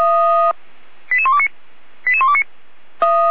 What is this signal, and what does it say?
Five-tone selective call (ZVEI), heard as quick runs of five tones, tones of about 80 ms. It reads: 88129 88129